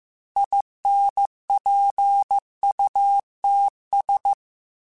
Morse code, heard as a keyed tone, about 15 wpm, 800 Hz. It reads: INPUTS
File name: Technical Subjects / Technical Subjects-mp3s / AND Gate @13